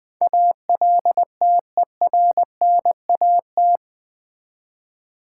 Morse code, heard as a keyed tone, about 19 wpm, 700 Hz. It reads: ALTERNAT
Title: Build A Word - 4 Letters At A Time - 11-Letter Words - 20wpm